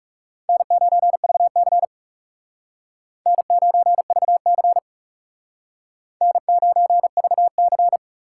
Morse code, heard as a keyed tone, about 40 wpm, 700 Hz. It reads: N9VC N9VC N9VC